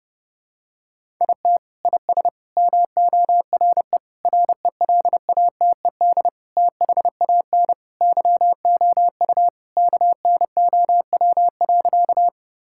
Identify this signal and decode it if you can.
Morse code, heard as a keyed tone, about 30 wpm, 700 Hz. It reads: IT IS MORE RELATED THAN YOU KNOW.
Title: Sentences from Top 1000 Words - Review Part 5 - 30wpm